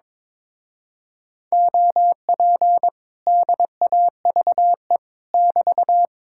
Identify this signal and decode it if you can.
Morse code, heard as a keyed tone, about 22 wpm, 700 Hz. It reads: OP DAVE <BT>
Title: Speed-Racing Common QSO Phrases - 22wpm